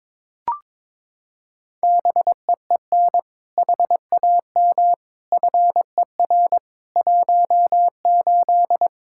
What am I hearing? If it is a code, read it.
Morse code, heard as a keyed tone, about 22 wpm, 700 Hz. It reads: BEEN HAM FER 18
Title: Practice QSO - Exchange 3 - More Info - 22wpm